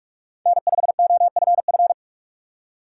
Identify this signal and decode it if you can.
Morse code, heard as a keyed tone, about 44 wpm, 700 Hz. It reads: N5OFF
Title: USA Call Signs - 2x Character Spacing - 45wpm